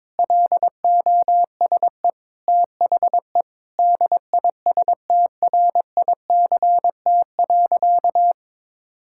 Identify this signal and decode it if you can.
Morse code, heard as a keyed tone, about 22 wpm, 700 Hz. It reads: LOSE THE DISTRICT.
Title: Sentences from Top 1600 Words 22wpm